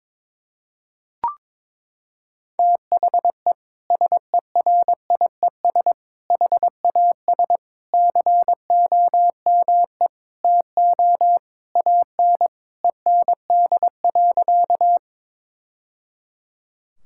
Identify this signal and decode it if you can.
Morse code, heard as a keyed tone, about 22 wpm, 700 Hz. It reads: THE SERIES HAS COME TO AN END.
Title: Sentences from Top 1000 Words - Review Part 5 - 22wpm